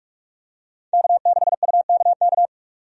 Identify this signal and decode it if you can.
Morse code, heard as a keyed tone, about 43 wpm, 700 Hz. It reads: K6UKK
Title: USA Call Signs - 5 Characters - 45wpm